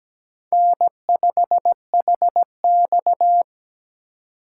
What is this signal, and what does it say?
Morse code, heard as a keyed tone, about 20 wpm, 700 Hz. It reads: N5HX
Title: Speed Racing - USA Call Signs 17wpm